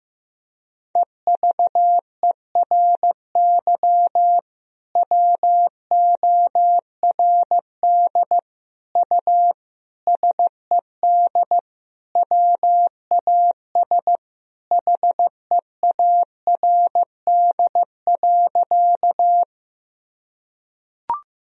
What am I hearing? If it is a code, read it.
Morse code, heard as a keyed tone, about 15 wpm, 700 Hz. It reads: EVERY WORD U SED WAS HEARD.
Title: Sentences from Top 400 Words and Abbreviations 15wpm